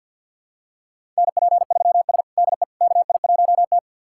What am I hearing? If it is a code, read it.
Morse code, heard as a keyed tone, about 50 wpm, 700 Hz. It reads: NP3S DE KI1T
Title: USA Callsign DE Callsign - 4 Characters - 50wpm